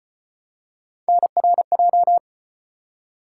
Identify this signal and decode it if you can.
Morse code, heard as a keyed tone, about 36 wpm, 700 Hz. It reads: NRJ